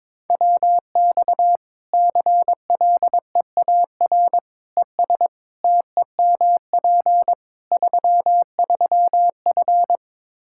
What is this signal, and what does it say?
Morse code, heard as a keyed tone, about 22 wpm, 700 Hz. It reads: WX CLEAR ES TEMP 33F
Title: Practice QSO - Exchange 2 - Info - 22wpm